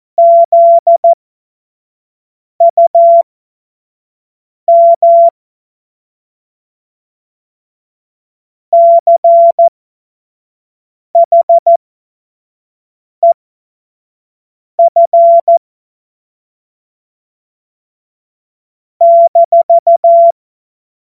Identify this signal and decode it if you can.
Morse code, heard as a keyed tone, about 14 wpm, spaced as for 5 wpm, 680 Hz. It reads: ZUM CHEF -
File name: Morsekurs / Witze / Witz 03. Januar 2023 25 F70